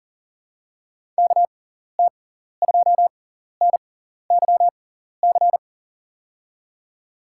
Morse code, heard as a keyed tone, about 40 wpm, spaced as for 14 wpm, 700 Hz. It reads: KT2NYC
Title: USA Call Signs - 6 Characters - 6x Character Spacing - 40wpm